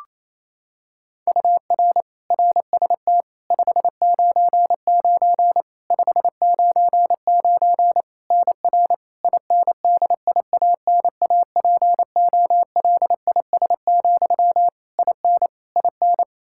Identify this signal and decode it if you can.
Morse code, heard as a keyed tone, about 28 wpm, 700 Hz. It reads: UR RST 599 599 NR INDIANAPOLIS, IN IN
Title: Practice QSO - Exchange 1 - Beginning - 28wpm